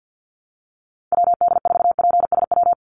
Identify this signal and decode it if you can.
Morse code, heard as a keyed tone, about 50 wpm, 700 Hz. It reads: WD4PSW
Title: USA Call Signs 50wpm